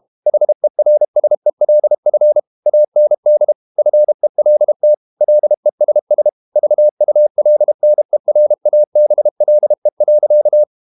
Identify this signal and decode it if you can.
Morse code, heard as a keyed tone, about 32 wpm, 600 Hz. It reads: HERSELF AND FELT LESS VULNERABLE.
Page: Never 31wpm | CW med Gnister